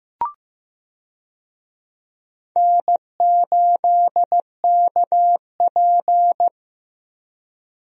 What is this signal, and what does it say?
Morse code, heard as a keyed tone, about 15 wpm, 700 Hz. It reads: N8KP
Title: USA Call Signs - 4 Characters - Repeated 2x - 15wpm